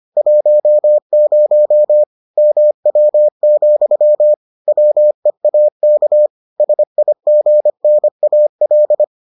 Morse code, heard as a keyed tone, about 25 wpm, 600 Hz. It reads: 10 MW, WEAK SIGNAL